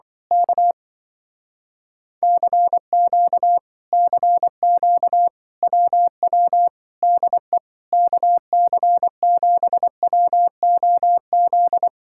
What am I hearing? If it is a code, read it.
Morse code, heard as a keyed tone, about 24 wpm, 700 Hz. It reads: K CQ CQ WW DE KC7WOZ
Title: Speed-Racing Common QSO Phrases Encore - 20wpm